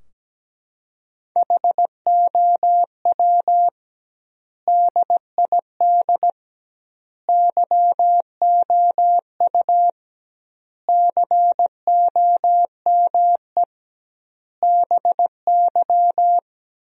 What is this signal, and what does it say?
Morse code, heard as a keyed tone, about 17 wpm, 700 Hz. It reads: HOW DID YOU COME BY
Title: 2x Word-Spacing - Sentences from Top 100 Words - 17wpm